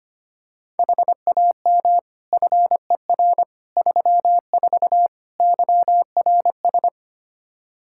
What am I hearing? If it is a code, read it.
Morse code, heard as a keyed tone, about 25 wpm, 700 Hz. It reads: HAM FER 34 YRS
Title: Practice QSO - Exchange 3 - More Info - 25wpm